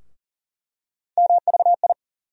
Morse code, heard as a keyed tone, about 40 wpm, 700 Hz. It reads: MVI
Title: USA Call Sign Postfixes - 3 Letters 40wpm